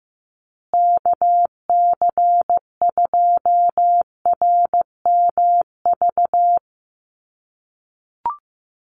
Morse code, heard as a keyed tone, about 15 wpm, 700 Hz. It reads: KC2RMV
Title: USA Call Signs Repeated 3x 15wpm